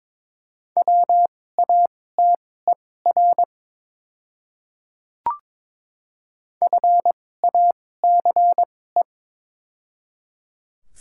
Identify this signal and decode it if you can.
Morse code, heard as a keyed tone, about 22 wpm, spaced as for 16 wpm, 700 Hz. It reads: WATER FACE
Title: 2x Character Spacing - Top 500 Words - 22wpm